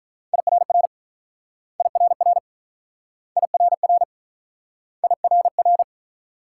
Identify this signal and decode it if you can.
Morse code, heard as a keyed tone, about 44 wpm, 700 Hz. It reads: IRR IRR IRR IRR